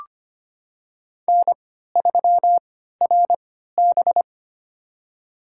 Morse code, heard as a keyed tone, about 25 wpm, spaced as for 14 wpm, 700 Hz. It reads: N3RB
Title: USA Call Signs - 4 Characters - 3x Character Spacing - 25wpm